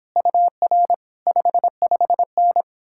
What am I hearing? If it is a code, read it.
Morse code, heard as a keyed tone, about 31 wpm, 700 Hz. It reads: UR 55N